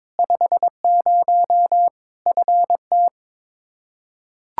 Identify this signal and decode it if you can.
Morse code, heard as a keyed tone, about 22 wpm, 700 Hz. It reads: 50 FT
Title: Practice QSO - Exchange 2 - Info - 22wpm